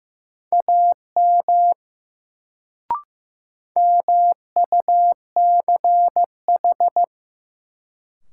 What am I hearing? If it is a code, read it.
Morse code, heard as a keyed tone, about 15 wpm, 700 Hz. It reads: AM MUCH
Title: Sets of 2 Words - New Word Drill - 15wpm